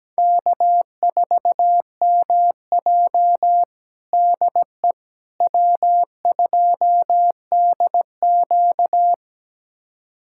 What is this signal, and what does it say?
Morse code, heard as a keyed tone, about 17 wpm, 700 Hz. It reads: K4MJ DE W2DQ
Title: USA Callsign DE Callsign - 4 Characters - 17wpm